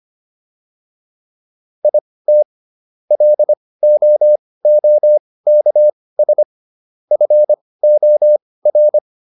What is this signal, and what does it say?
Morse code, heard as a keyed tone, about 25 wpm, spaced as for 18 wpm, 600 Hz. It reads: IT LOOKS FOR